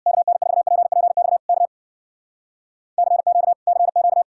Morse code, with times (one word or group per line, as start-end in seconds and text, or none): 0.1-1.4 KN4PPX
1.5-1.7 K
3.0-3.5 CQ
3.7-4.2 CQ